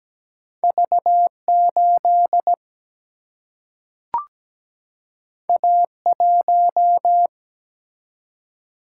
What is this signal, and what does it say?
Morse code, heard as a keyed tone, about 17 wpm, 700 Hz. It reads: V8 A1
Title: Single-Letter Single-Number 17wpm